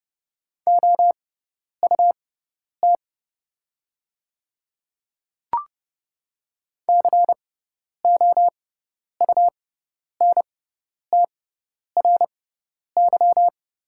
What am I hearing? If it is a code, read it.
Morse code, heard as a keyed tone, about 30 wpm, spaced as for 10 wpm, 700 Hz. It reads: OUT COUNTRY
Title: 6x Character Spacing - Top 300 Words - 30wpm